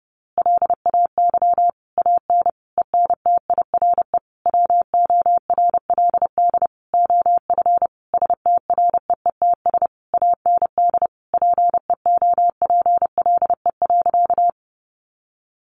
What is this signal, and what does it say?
Morse code, heard as a keyed tone, about 30 wpm, 700 Hz. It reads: LAY AN ENTIRE WORLD OF STREETS AND PEOPLE.
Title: A Sound Of Thunder 30wpm